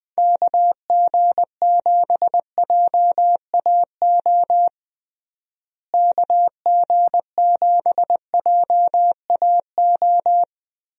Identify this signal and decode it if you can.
Morse code, heard as a keyed tone, about 20 wpm, 700 Hz. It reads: KG7JAO KG7JAO
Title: USA Call Signs - 6 Characters - Repeated 3x - 20wpm